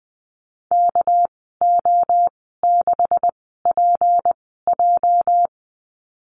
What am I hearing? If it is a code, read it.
Morse code, heard as a keyed tone, about 20 wpm, spaced as for 14 wpm, 700 Hz. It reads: KO6PJ
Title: 213 Lesson 34g - Number 8 - Call Signs 20wpm